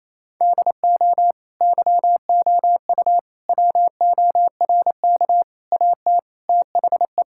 Morse code, heard as a keyed tone, about 28 wpm, 700 Hz. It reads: DO YOU WORK AT THE
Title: Sentences from Top 1000 Words - Review Part 5 - 28wpm